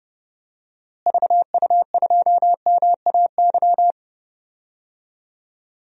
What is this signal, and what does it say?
Morse code, heard as a keyed tone, about 31 wpm, 700 Hz. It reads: VU2MAY
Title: DXCC-100 Call Signs - 6 Characters - Repeated 2x - 30wpm